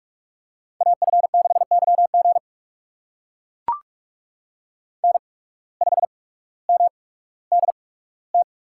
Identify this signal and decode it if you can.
Morse code, heard as a keyed tone, about 44 wpm, spaced as for 19 wpm, 700 Hz. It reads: AF6YG N5MDT
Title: USA Call Signs - 5 Characters - 8x Character Spacing - 45wpm